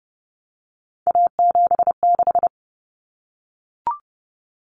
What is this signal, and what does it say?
Morse code, heard as a keyed tone, about 30 wpm, 700 Hz. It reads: A76
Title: Common Cut Numbers - 3-Digit 30wpm